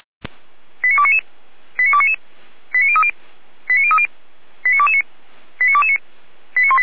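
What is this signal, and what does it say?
Five-tone selective call (ZVEI), heard as quick runs of five tones, tones of about 70 ms. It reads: 89299 89299 89030 89030 89229 89229
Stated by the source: ZVEI five-tone selective call 89250 received on SDR, hosted by feuerwehr-orlamuende.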